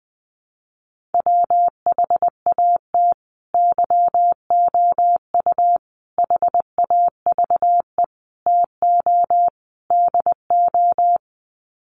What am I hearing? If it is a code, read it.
Morse code, heard as a keyed tone, about 20 wpm, 700 Hz. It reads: WHAT YOU HAVE TO DO
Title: Sets of 5 Words Encore 20wpm